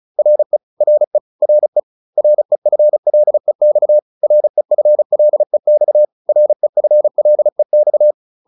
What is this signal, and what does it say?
Morse code, heard as a keyed tone, about 34 wpm, 600 Hz. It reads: RE RE RE REFLEX REFLEX REFLEX